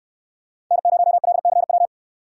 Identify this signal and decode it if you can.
Morse code, heard as a keyed tone, about 68 wpm, 700 Hz. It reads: N0LCW